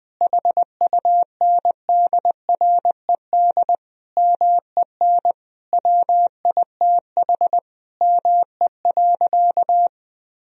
Morse code, heard as a keyed tone, about 20 wpm, 700 Hz. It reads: HUNDRED MEN WITH ME.
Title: Sentences from Top 1000 Words - Review Part 2 - 20wpm